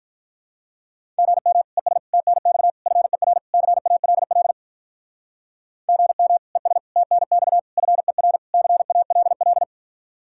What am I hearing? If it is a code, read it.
Morse code, heard as a keyed tone, about 45 wpm, 700 Hz. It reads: GM ES TNX FER CALL GM ES TNX FER CALL